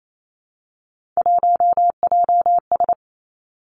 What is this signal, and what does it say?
Morse code, heard as a keyed tone, about 28 wpm, 690 Hz. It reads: 1JS
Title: Three Challenging Letter-Number Combinations 28wpm